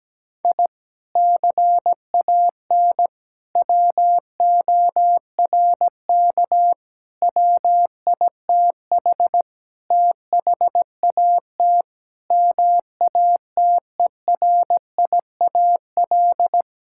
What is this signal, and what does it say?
Morse code, heard as a keyed tone, about 17 wpm, 700 Hz. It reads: I CAN WORK WITH THAT MATERIAL
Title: Sentences from Top 1000 Words - Review Part 5 - 17wpm